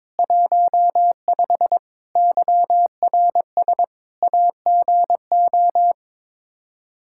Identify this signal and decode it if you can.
Morse code, heard as a keyed tone, about 22 wpm, 700 Hz. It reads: 15 YRS AGO
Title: Practice QSO - Exchange 3 - More Info - 22wpm